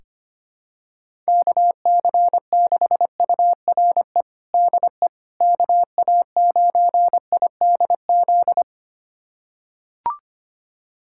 Morse code, heard as a keyed tone, about 25 wpm, 700 Hz. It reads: KC6URE DE KA9IDZ
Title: USA Callsign DE Callsign - 6 Characters - 25wpm